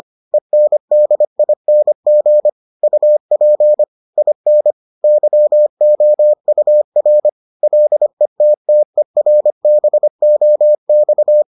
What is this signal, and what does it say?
Morse code, heard as a keyed tone, about 25 wpm, 600 Hz. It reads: ENDING UP IN YOUR LETTERBOX